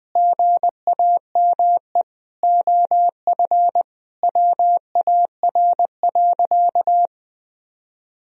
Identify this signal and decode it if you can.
Morse code, heard as a keyed tone, about 20 wpm, 700 Hz. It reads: GAME OF WAR.
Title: Sentences from Top 1000 Words - Review Part 3 - 20wpm